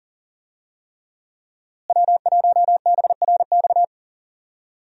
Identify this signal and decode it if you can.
Morse code, heard as a keyed tone, about 40 wpm, 700 Hz. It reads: W1BRX
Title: USA Call Signs - 5 Characters - Repeated 3x - 40wpm